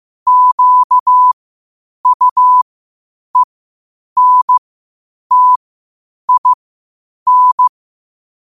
Morse code, heard as a keyed tone, about 15 wpm, spaced as for 8 wpm, 1000 Hz. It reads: QUENTIN
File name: qotdfiles / 8wpm